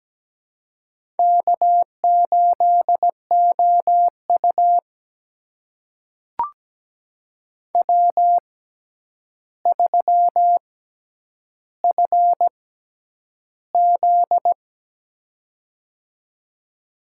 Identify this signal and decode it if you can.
Morse code, heard as a keyed tone, about 17 wpm, spaced as for 9 wpm, 700 Hz. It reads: K8OU W3FZ
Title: USA Call Signs - 4 Characters - 6x Character Spacing - 17wpm